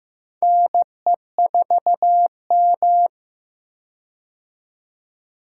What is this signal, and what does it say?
Morse code, heard as a keyed tone, about 15 wpm, 700 Hz. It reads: NE4M
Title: K1USN SST QSO Practice - 15wpm